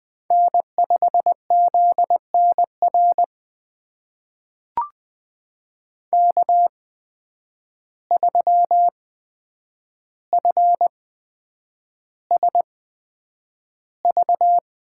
Morse code, heard as a keyed tone, about 20 wpm, spaced as for 9 wpm, 700 Hz. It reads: N5ZNR K3FSV
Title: USA Call Signs - 5 Characters - 8x Character Spacing - 20wpm